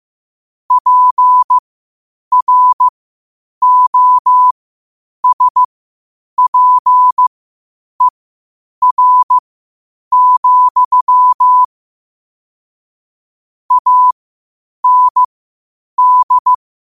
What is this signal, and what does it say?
Morse code, heard as a keyed tone, about 15 wpm, spaced as for 8 wpm, 1000 Hz. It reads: PROSPER, AND
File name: qotdfiles / 8wpm